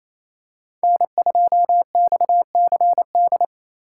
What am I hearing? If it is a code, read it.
Morse code, heard as a keyed tone, about 28 wpm, 700 Hz. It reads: N2XCD
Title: USA Call Signs - 4x Character Spacing - 28wpm